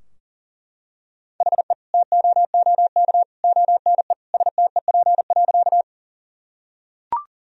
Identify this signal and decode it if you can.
Morse code, heard as a keyed tone, about 40 wpm, 700 Hz. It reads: HE TOOK ONE STEP.
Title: Sentences from Top 1000 Words - Review Part 2 - 40wpm